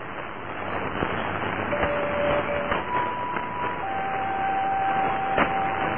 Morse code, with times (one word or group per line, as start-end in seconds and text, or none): none